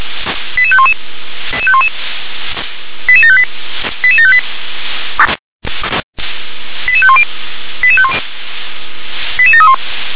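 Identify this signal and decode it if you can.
Five-tone selective call (ZVEI), heard as quick runs of five tones, tones of about 70 ms. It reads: 88511 88758 88758 88510 88731
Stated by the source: ZVEI five-tone call 88730 received by SDR.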